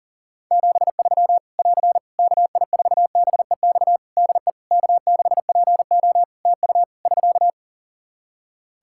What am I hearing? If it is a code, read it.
Morse code, heard as a keyed tone, about 40 wpm, 700 Hz. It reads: 73 <AR> KI4BEX DE K6PO TU <SK>
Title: Practice QSO - Exchange 4 - Ending - 40wpm